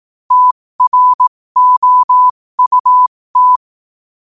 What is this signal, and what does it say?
Morse code, heard as a keyed tone, about 18 wpm, spaced as for 15 wpm, 1000 Hz. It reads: TROUT
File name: qotdfiles / 15wpm